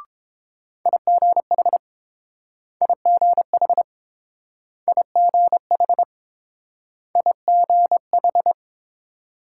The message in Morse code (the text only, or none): IGH IGH IGH IGH